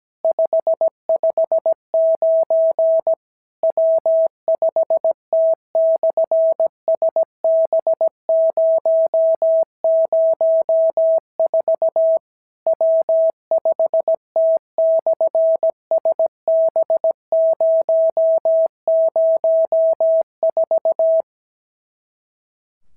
Morse code, heard as a keyed tone, about 17 wpm, 650 Hz. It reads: 559 W5T/SB004 W5T/SB004
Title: SOTA QSO Practice 17wpm